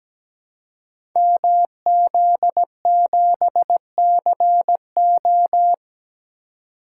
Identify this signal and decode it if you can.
Morse code, heard as a keyed tone, about 17 wpm, 700 Hz. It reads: MZ7CO
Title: Code Groups - 5 Letter:Numbers - 17wpm